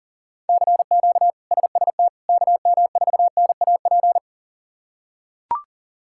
Morse code, heard as a keyed tone, about 40 wpm, 700 Hz. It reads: CQ SST KM4NAP
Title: K1USN SST QSO Practice - 40wpm